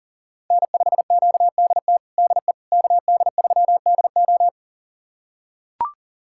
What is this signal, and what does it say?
Morse code, heard as a keyed tone, about 40 wpm, 700 Hz. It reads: N5QDT DE KD3DO
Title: USA Callsign DE Callsign - 5 Characters - 40wpm